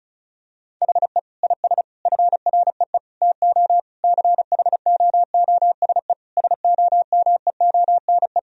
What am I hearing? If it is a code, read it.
Morse code, heard as a keyed tone, about 36 wpm, 700 Hz. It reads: HE IS FREE TO CHOOSE SOMEONE
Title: Sentences from Top 1000 Words - Review Part 4 - 35wpm